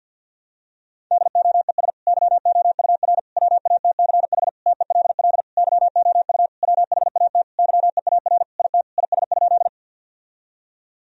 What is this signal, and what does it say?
Morse code, heard as a keyed tone, about 52 wpm, 700 Hz. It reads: DOES YOUR WATCH TELL YOU WHAT YEAR IT IS?
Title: Sentences from Top 1000 Words - Review Part 1 - 50wpm